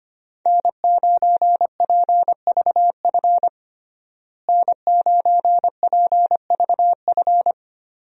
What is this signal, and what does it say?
Morse code, heard as a keyed tone, about 25 wpm, 700 Hz. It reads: N9PVF N9PVF
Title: USA Call Signs - 5 Characters - Repeated 2x - 25wpm